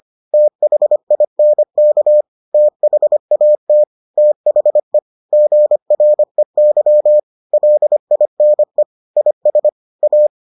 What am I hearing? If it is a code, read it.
Morse code, heard as a keyed tone, about 25 wpm, 600 Hz. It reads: THINK THAT THE GREY LINE IS A